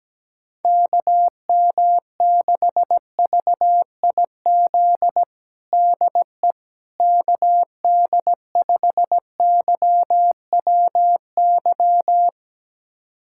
Morse code, heard as a keyed tone, about 17 wpm, 700 Hz. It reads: KM6VIZ DE KD5YWY
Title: USA Callsign DE Callsign - 6 Characters - 17wpm